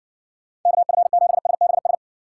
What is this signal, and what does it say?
Morse code, heard as a keyed tone, about 55 wpm, 700 Hz. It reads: KF7SBS